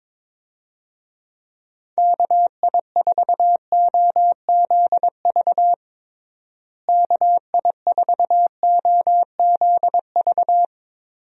Morse code, heard as a keyed tone, about 21 wpm, 700 Hz. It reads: KI4OZV KI4OZV
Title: USA Call Signs - 6 Characters - Repeated 3x - 22wpm